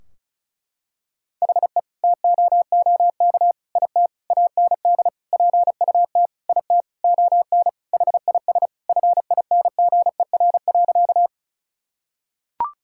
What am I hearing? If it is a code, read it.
Morse code, heard as a keyed tone, about 34 wpm, 700 Hz. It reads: HE TOOK IT AND PUT IT ON HIS FINGER.
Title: Sentences from Top 1600 Words 35wpm